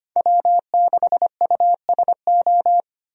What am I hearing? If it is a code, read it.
Morse code, heard as a keyed tone, about 25 wpm, 700 Hz. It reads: W6USO
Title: USA Call Signs - 5 Characters - 25wpm